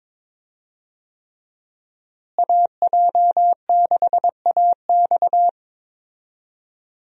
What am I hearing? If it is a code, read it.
Morse code, heard as a keyed tone, about 22 wpm, 700 Hz. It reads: AJ6AX
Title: USA Call Signs - 5 Characters - Repeated 3x - 22wpm